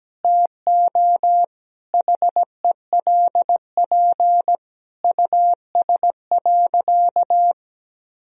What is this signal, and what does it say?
Morse code, heard as a keyed tone, about 17 wpm, 700 Hz. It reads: TO HELP US.
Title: Sentences from Top 1000 Words - Review Part 2 - 17wpm